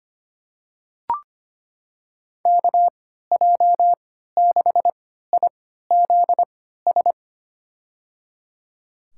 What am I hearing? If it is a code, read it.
Morse code, heard as a keyed tone, about 25 wpm, spaced as for 14 wpm, 700 Hz. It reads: KJ6IZS
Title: USA Call Signs - 6 Characters - 3x Character Spacing - 25wpm